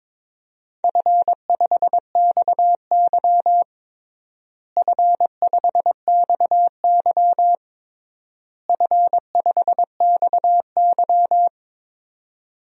Morse code, heard as a keyed tone, about 22 wpm, 700 Hz. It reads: F5XY F5XY F5XY
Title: DXCC-100 Call Signs - 4 Characters - Repeated 3x - 22wpm